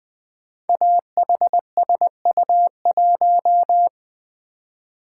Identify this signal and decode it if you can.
Morse code, heard as a keyed tone, about 21 wpm, 700 Hz. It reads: AHSU1
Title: Code Groups - 5 Letter:Numbers - 20wpm